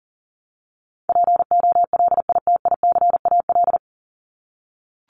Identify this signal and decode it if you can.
Morse code, heard as a keyed tone, about 40 wpm, 700 Hz. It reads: POLITICAL
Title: Top 501-750 Words 40wpm